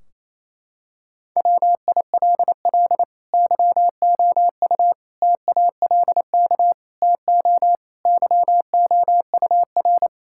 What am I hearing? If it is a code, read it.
Morse code, heard as a keyed tone, about 28 wpm, 700 Hz. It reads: WILL YOU TALK TO YOUR
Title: Sentences from Top 1600 Words 28wpm